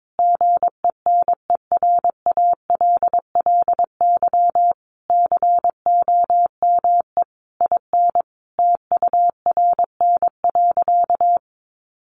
Morse code, heard as a keyed tone, about 22 wpm, 700 Hz. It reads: GENERALLY COME IN TURN.
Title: Sentences from Top 1500 Words 22wpm